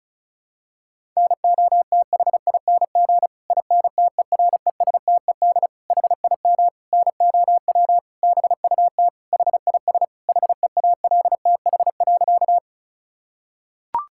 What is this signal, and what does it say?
Morse code, heard as a keyed tone, about 34 wpm, 700 Hz. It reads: NOTHING INTERESTED HIM NOW BUT HIS HEALTH.
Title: Sentences from Top 1000 Words - Review Part 6 - 35wpm